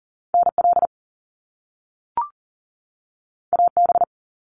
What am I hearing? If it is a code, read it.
Morse code, heard as a keyed tone, about 40 wpm, 700 Hz. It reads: NL AB
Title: Canadian Provinces and Territories 40wpm